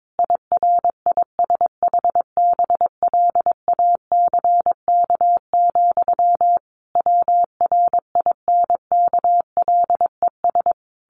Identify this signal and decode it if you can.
Morse code, heard as a keyed tone, about 22 wpm, 700 Hz. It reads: IRISHBLACK, WRINKLES
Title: A Sound Of Thunder 22wpm